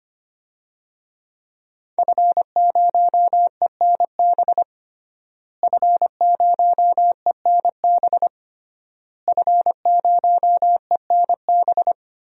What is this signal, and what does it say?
Morse code, heard as a keyed tone, about 25 wpm, 700 Hz. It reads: F0ENB F0ENB F0ENB
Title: DXCC-100 Call Signs - 5 Characters - Repeated 3x - 25wpm